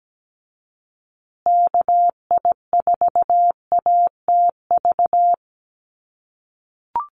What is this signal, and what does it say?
Morse code, heard as a keyed tone, about 17 wpm, 700 Hz. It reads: KI4ATV
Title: USA Call Signs Repeated 3x 17wpm